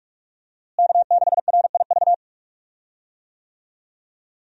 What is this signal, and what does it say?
Morse code, heard as a keyed tone, about 45 wpm, 700 Hz. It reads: K6RIV
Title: USA Call Signs - 5 Characters - Repeated 3x - 45wpm